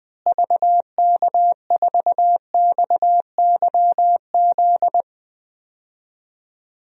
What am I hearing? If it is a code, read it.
Morse code, heard as a keyed tone, about 20 wpm, 700 Hz. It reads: VK4XYZ
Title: DXCC-100 Call Signs - 6 Characters - Repeated 3x - 20wpm